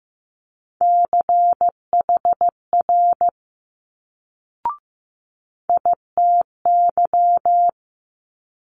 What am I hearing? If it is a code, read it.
Morse code, heard as a keyed tone, about 15 wpm, 700 Hz. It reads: CHR ITY
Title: Multi-letter Phonemes Quick Fire 15wpm